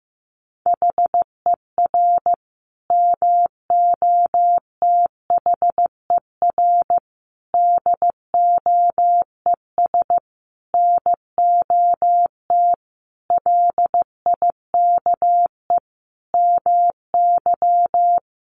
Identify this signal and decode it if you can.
Morse code, heard as a keyed tone, about 15 wpm, 700 Hz. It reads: HER MOTHER DOES NOT LIKE MY
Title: Sentences from Top 1400 Words 15wpm